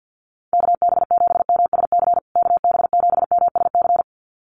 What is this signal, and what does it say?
Morse code, heard as a keyed tone, about 50 wpm, 700 Hz. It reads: KB7MSC KB7MSC
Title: USA Call Signs Repeated 2x 50wpm